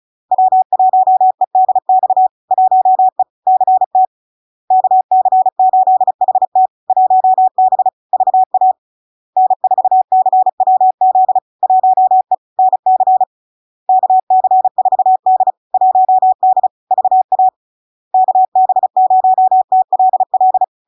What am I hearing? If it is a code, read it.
Morse code, heard as a keyed tone, about 36 wpm, 750 Hz. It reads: W1EDX 1E CT KC8HT 1B VA N4CWZ 1E NC KC4D 1D VA KB0TLL
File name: LARK-FD-2022-Log-in-CW-35wpm-750-Hz